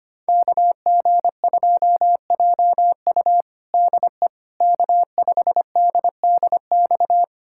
Morse code, heard as a keyed tone, about 25 wpm, 700 Hz. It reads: KG2JU DE K5DDX
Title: USA Callsign DE Callsign - 5 Characters - 25wpm